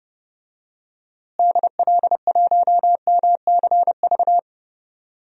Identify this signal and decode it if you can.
Morse code, heard as a keyed tone, about 32 wpm, 700 Hz. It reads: DL1MCV
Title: DXCC-100 Call Signs - 6 Characters - Repeated 3x - 30wpm